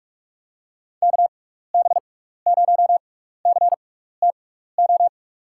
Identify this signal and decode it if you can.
Morse code, heard as a keyed tone, about 46 wpm, spaced as for 16 wpm, 700 Hz. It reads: KB0CTO
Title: USA Call Signs - 6 Characters - 6x Character Spacing - 45wpm